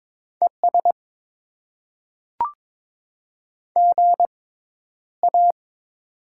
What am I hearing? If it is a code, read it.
Morse code, heard as a keyed tone, about 22 wpm, spaced as for 11 wpm, 700 Hz. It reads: ES GA
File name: Top 20 QSO Elements - 6x Character Spacing - 22wpm